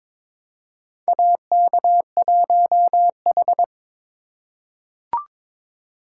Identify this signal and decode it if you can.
Morse code, heard as a keyed tone, about 22 wpm, 700 Hz. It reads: AK1H
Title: USA Call Signs - 2x Character Spacing - 22wpm